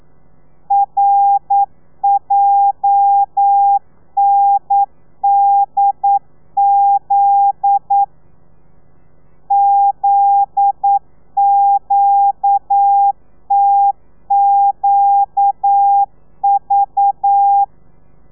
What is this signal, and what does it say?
Morse code, heard as a keyed tone, about 9 wpm, 800 Hz. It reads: RJNDZ ZQTQV
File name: telegrafia / i2tfj / Lez9 40